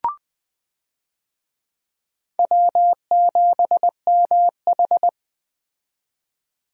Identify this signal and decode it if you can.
Morse code, heard as a keyed tone, about 20 wpm, 700 Hz. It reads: W7MH
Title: USA Call Signs - 4 Characters - 20wpm